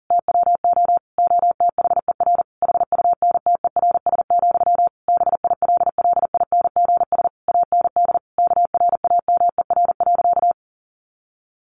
Morse code, heard as a keyed tone, about 40 wpm, 700 Hz. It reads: TWO OTHER HUNTERS, BILLINGS AND KRAMER.
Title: A Sound Of Thunder 40wpm